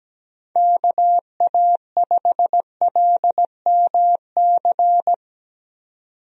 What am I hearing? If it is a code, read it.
Morse code, heard as a keyed tone, about 17 wpm, 700 Hz. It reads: KA5LMC
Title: USA Call Signs - 6 Characters - 17wpm